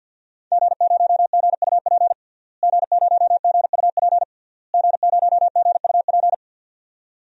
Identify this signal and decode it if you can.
Morse code, heard as a keyed tone, about 49 wpm, 700 Hz. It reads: G0GUP G0GUP G0GUP
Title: DXCC-100 Call Signs - 5 Characters - Repeated 3x - 50wpm